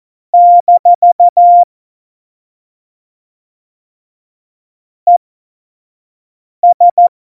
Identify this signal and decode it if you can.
Morse code, heard as a keyed tone, about 14 wpm, spaced as for 5 wpm, 700 Hz. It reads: - ES
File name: Morsekurs / Witze / Witz 02. Juli 2019, 25 F70